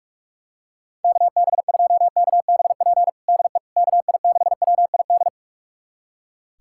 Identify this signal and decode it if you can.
Morse code, heard as a keyed tone, about 46 wpm, 690 Hz. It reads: KB2KBP DE KI6WID
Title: USA Callsign DE Callsign - 6 Characters - 45wpm